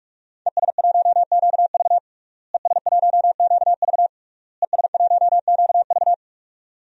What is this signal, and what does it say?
Morse code, heard as a keyed tone, about 46 wpm, 700 Hz. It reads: ES1QV ES1QV ES1QV
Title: DXCC-100 Call Signs - 5 Characters - Repeated 3x - 45wpm